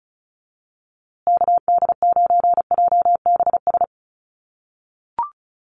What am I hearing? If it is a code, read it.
Morse code, heard as a keyed tone, about 34 wpm, 700 Hz. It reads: KD9JBS
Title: USA Call Signs Repeated 3x 35wpm